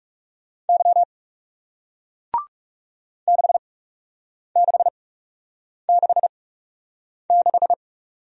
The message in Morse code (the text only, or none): Y 6666